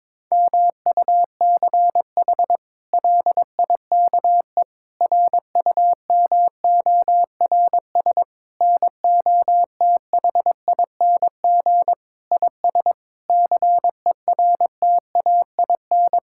Morse code, heard as a keyed tone, about 21 wpm, 700 Hz. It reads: MUCH LIKE RUMORS NOTHING IS CERTAIN
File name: practice / 22wpm / Sets of 8 Words - 22wpm